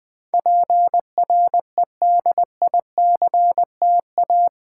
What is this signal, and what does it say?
Morse code, heard as a keyed tone, about 20 wpm, 700 Hz. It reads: PREDICTA
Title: Build A Word - 4 Letters At A Time - 11-Letter Words - 20wpm